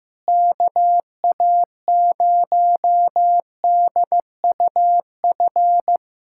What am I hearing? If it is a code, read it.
Morse code, heard as a keyed tone, about 15 wpm, 700 Hz. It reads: KA0DUF
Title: USA Callsign DE Callsign 15wpm